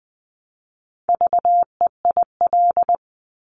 Morse code, heard as a keyed tone, about 20 wpm, 700 Hz. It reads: VEIL
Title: Multi-letter Phonemes Repeated 3x 20wpm